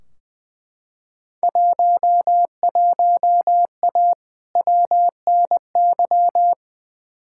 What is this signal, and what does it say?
Morse code, heard as a keyed tone, about 20 wpm, 700 Hz. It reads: 11A WNY
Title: Field Day Exchanges Encore - 20wpm